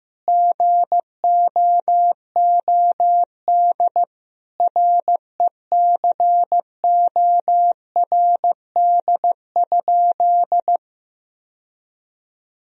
Morse code, heard as a keyed tone, about 15 wpm, 700 Hz. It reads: GOOD RECORD?
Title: Sentences from Top 1000 Words - Review Part 3 - 15wpm